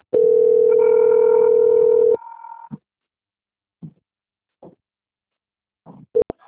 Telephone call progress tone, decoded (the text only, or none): ringback tone